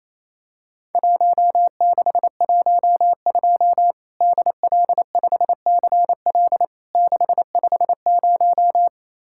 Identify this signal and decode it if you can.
Morse code, heard as a keyed tone, about 28 wpm, 700 Hz. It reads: 1612 DL5CL 650
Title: Worked All Europe DX Contest - QTC Exchange - 28wpm